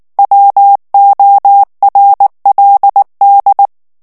Morse code, heard as a keyed tone, about 19 wpm, 800 Hz. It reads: WORLD
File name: morse mp3 / ALLEMAND texte clair 2 vitesse 18 mots minute